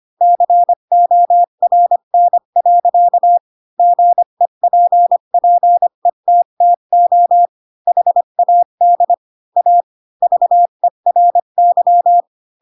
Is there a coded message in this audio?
Yes: Morse code from 0.2 to 12.2 s, heard as a keyed tone, about 25 wpm, 700 Hz. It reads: CORN. GEPPETTO HAD A VERY